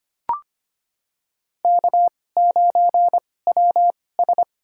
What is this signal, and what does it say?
Morse code, heard as a keyed tone, about 25 wpm, spaced as for 18 wpm, 700 Hz. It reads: K9WS
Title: USA Call Signs - 4 Characters - 2x Character Spacing - 25wpm